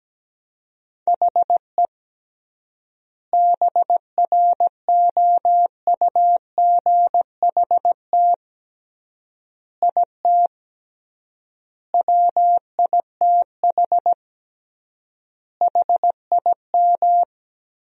Morse code, heard as a keyed tone, about 17 wpm, 700 Hz. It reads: HE BROUGHT IT WITH HIM
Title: 3x Word-Spacing - Sentences from Top 500 Words - 17wpm